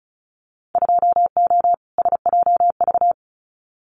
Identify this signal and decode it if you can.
Morse code, heard as a keyed tone, about 34 wpm, 700 Hz. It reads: 2O SJV
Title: Winter Field Day Exchanges Encore 35wpm